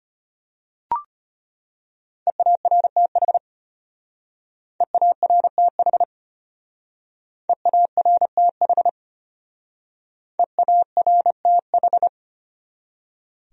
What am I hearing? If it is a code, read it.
Morse code, heard as a keyed tone, about 30 wpm, 700 Hz. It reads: EARTH EARTH EARTH EARTH
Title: Speed-Racing Top 200 Words 25wpm